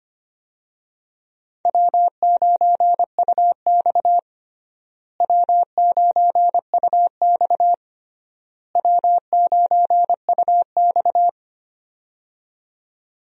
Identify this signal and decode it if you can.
Morse code, heard as a keyed tone, about 25 wpm, 700 Hz. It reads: W9UX W9UX W9UX
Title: USA Call Signs - 4 Characters - Repeated 3x - 25wpm